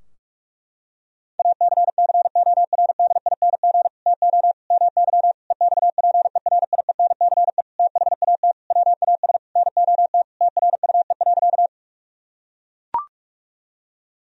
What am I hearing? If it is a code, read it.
Morse code, heard as a keyed tone, about 43 wpm, 700 Hz. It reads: ACCORDING TO MY EXPERIENCE THAT WAS NOT TRUE.
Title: Sentences from Top 1000 Words - Review Part 4 - 45wpm